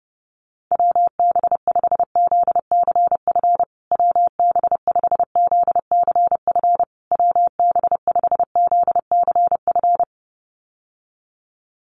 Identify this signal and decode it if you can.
Morse code, heard as a keyed tone, about 30 wpm, 700 Hz. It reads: WB5ZCF WB5ZCF WB5ZCF
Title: USA Call Signs Repeated 3x 30wpm